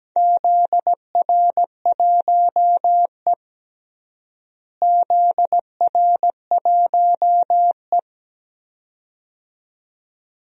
Morse code, heard as a keyed tone, about 17 wpm, 700 Hz. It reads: ZR1E ZR1E
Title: DXCC-100 Call Signs - 4 Characters - Repeated 2x - 17wpm